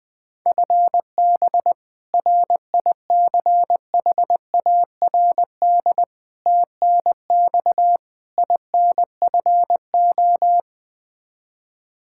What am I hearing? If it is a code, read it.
Morse code, heard as a keyed tone, about 20 wpm, 700 Hz. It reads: FB RICHARD TNX INFO
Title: Practice QSO - Exchange 3 - More Info - 20wpm